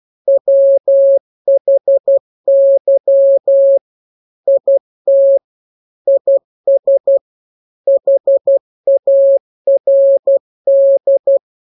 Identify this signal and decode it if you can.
Morse code, heard as a keyed tone, about 12 wpm, 550 Hz. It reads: WHY IT IS HARD
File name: jokes / joke 5 12 0000